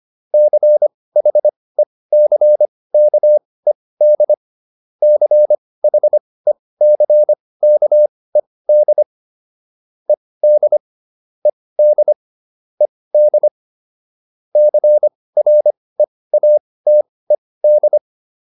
Morse code, heard as a keyed tone, about 25 wpm, spaced as for 18 wpm, 600 Hz. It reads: CHECKED CHECKED ED ED ED CREATED
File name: ED7 18